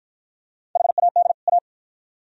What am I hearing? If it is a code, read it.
Morse code, heard as a keyed tone, about 50 wpm, 700 Hz. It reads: HAD A